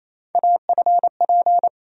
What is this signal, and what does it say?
Morse code, heard as a keyed tone, about 28 wpm, 700 Hz. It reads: AFP
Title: USA Call Sign Postfixes - 3 Letters 28wpm